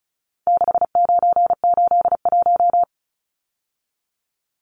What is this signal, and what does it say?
Morse code, heard as a keyed tone, about 34 wpm, 700 Hz. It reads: 6981
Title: Numbers - 4-Digit 35wpm